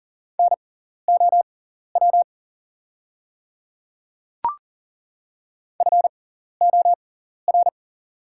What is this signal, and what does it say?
Morse code, heard as a keyed tone, about 40 wpm, spaced as for 14 wpm, 700 Hz. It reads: NOW FOR
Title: 6x Character Spacing - Top 100 Words - 40wpm